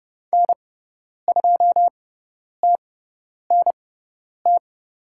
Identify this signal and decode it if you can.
Morse code, heard as a keyed tone, about 30 wpm, spaced as for 10 wpm, 700 Hz. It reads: N2TNT
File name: Lesson 16g - Letter c - Call Signs 30wpm